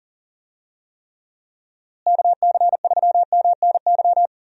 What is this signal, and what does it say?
Morse code, heard as a keyed tone, about 40 wpm, 700 Hz. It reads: KC3MNY
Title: USA Call Signs - 6 Characters - Repeated 2x - 40wpm